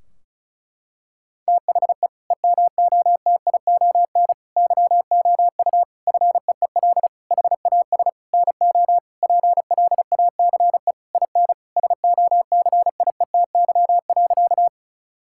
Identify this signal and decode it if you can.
Morse code, heard as a keyed tone, about 34 wpm, 700 Hz. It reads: THE EMOTION YOU FEEL HAS NO PLACE IN SOCIETY.
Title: Sentences from Top 1600 Words 35wpm